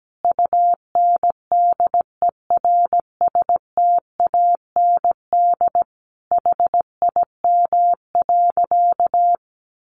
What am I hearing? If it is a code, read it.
Morse code, heard as a keyed tone, about 17 wpm, 700 Hz. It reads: UNDERSTAND HIM.
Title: Sentences from Top 600 Words 17wpm